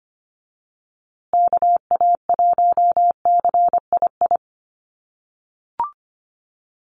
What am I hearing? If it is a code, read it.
Morse code, heard as a keyed tone, about 25 wpm, 700 Hz. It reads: KA1CII
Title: USA Call Signs Repeated 3x 25wpm